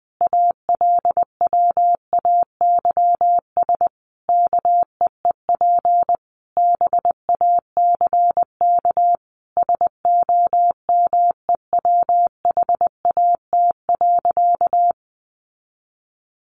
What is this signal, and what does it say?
Morse code, heard as a keyed tone, about 20 wpm, 700 Hz. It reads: ALWAYS KEEP BACK SOMEWHAT.
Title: Sentences from Top 1300 Words 20wpm